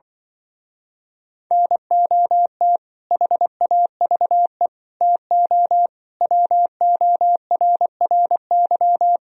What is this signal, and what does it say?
Morse code, heard as a keyed tone, about 24 wpm, 700 Hz. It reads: NOT HAVE TO WORRY
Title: Speed-Racing - Sets of 5 Words 20wpm